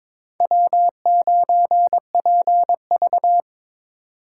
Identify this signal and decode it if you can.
Morse code, heard as a keyed tone, about 26 wpm, 700 Hz. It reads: W9PV
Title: Speed Racing - USA 1x2 Call Signs 22wpm